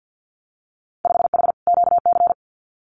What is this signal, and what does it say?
Morse code, heard as a keyed tone, about 50 wpm, 700 Hz. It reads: 5H QC
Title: Winter Field Day Exchanges Encore 50wpm